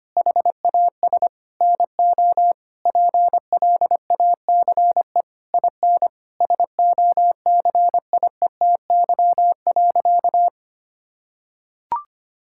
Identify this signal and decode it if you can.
Morse code, heard as a keyed tone, about 25 wpm, 700 Hz. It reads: HAS NO PLACE IN SOCIETY.
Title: Sentences from Top 1600 Words 25wpm